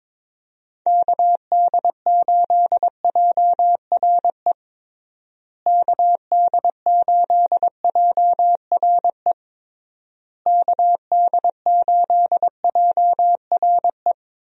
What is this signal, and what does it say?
Morse code, heard as a keyed tone, about 22 wpm, 700 Hz. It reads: KD8JRE KD8JRE KD8JRE
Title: USA Call Signs - 6 Characters - Repeated 3x - 22wpm